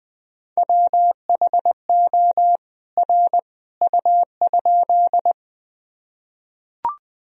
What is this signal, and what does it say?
Morse code, heard as a keyed tone, about 20 wpm, 700 Hz. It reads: WHO R U?
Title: Sentences from Top 100 Words and Abbreviations 20wpm